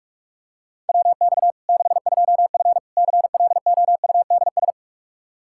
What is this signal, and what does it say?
Morse code, heard as a keyed tone, about 43 wpm, 700 Hz. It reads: WX 62F CLOUDS
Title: K1USN SST QSO Practice 45wpm